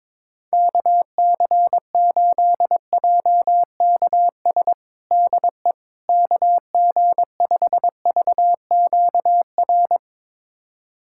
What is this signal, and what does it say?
Morse code, heard as a keyed tone, about 22 wpm, 700 Hz. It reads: KC8JKS DE KG5VQR
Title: USA Callsign DE Callsign - 6 Characters - 22wpm